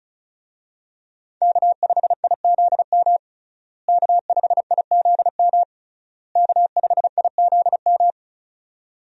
Morse code, heard as a keyed tone, about 36 wpm, 700 Hz. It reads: K5IZM K5IZM K5IZM
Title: USA Call Signs - 5 Characters - Repeated 3x - 35wpm